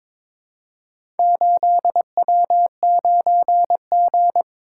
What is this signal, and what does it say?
Morse code, heard as a keyed tone, about 23 wpm, 700 Hz. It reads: 8W9G
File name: practice / 22wpm / Four Letter-Number Combinations - 22wpm